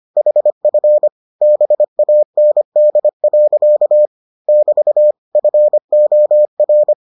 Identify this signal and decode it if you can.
Morse code, heard as a keyed tone, about 25 wpm, 600 Hz. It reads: HF BAND. <BT> FOR